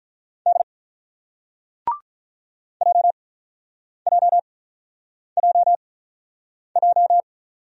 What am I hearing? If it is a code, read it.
Morse code, heard as a keyed tone, about 49 wpm, spaced as for 9 wpm, 700 Hz. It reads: D JJJJ